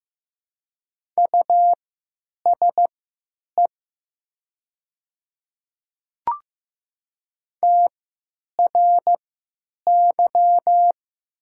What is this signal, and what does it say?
Morse code, heard as a keyed tone, about 15 wpm, spaced as for 9 wpm, 700 Hz. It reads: USE TRY
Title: 3x Character Spacing - Top 300 Words - 15wpm